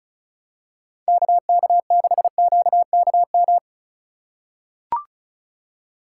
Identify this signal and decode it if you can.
Morse code, heard as a keyed tone, about 34 wpm, 700 Hz. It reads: KK6QKM
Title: USA Call Signs - 6 Characters - 3x Character Spacing - 35wpm